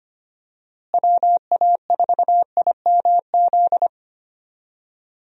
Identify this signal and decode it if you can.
Morse code, heard as a keyed tone, about 25 wpm, 700 Hz. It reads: WA4IMZ
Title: USA Call Signs - 6 Characters - Repeated 3x - 25wpm